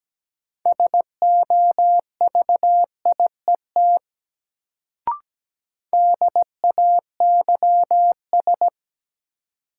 Morse code, heard as a keyed tone, about 17 wpm, 700 Hz. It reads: SOVIET DAYS
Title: Sets of 5 Words Encore - New Word Drill - 17wpm